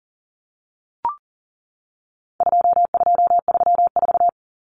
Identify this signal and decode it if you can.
Morse code, heard as a keyed tone, about 40 wpm, 700 Hz. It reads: 2234
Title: Common Cut Numbers - 4-Digit 40wpm